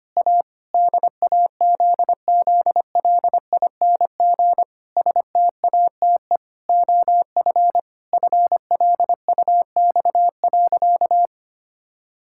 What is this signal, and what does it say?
Morse code, heard as a keyed tone, about 25 wpm, 700 Hz. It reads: A DAZZLING STATE OF FLUX.
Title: Pangram Challenge v2 - 25wpm